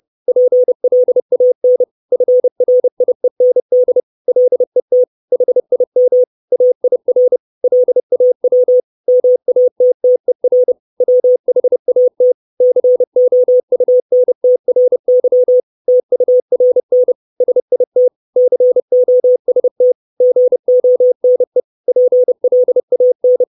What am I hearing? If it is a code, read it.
Morse code, heard as a keyed tone, about 30 wpm, 500 Hz. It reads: PLAN FRIEND LET HIM AIR LAW MATTER WHAT COUNTRY TURN SIT COST GONE PLAN
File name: 632 Words 30WPM